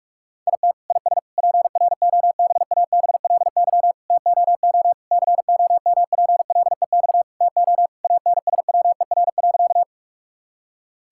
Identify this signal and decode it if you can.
Morse code, heard as a keyed tone, about 46 wpm, 700 Hz. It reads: IT IS PROBABLY TOO COMPLEX TO ANSWER.
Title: Sentences from Top 1000 Words - Review Part 6 - 45wpm